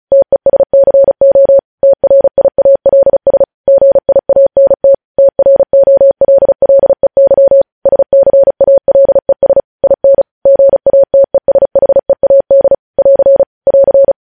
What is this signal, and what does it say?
Morse code, heard as a keyed tone, about 36 wpm, 550 Hz. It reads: TESCO TRIALS GIANT TROLLEY SCALES IN GATESHEAD <AR> <AR>
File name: podcast / News headlines-1x-35x35